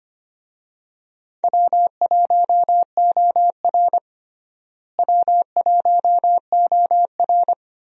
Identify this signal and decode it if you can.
Morse code, heard as a keyed tone, about 25 wpm, 700 Hz. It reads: W1OR W1OR
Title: USA Call Signs - 4 Characters - Repeated 3x - 25wpm